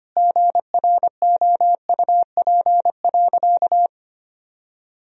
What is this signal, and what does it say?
Morse code, heard as a keyed tone, about 25 wpm, 700 Hz. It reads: GROUP.
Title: Sentences from Top 1000 Words - Review Part 1 - 25wpm